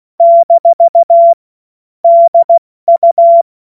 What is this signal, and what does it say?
Morse code, heard as a keyed tone, about 16 wpm, 680 Hz. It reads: - DU
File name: Morsekurs / Witze / Witz 14. März 2023 70 F80